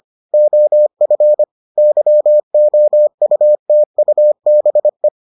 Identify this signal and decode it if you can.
Morse code, heard as a keyed tone, about 25 wpm, 600 Hz. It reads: OF YOUTUBE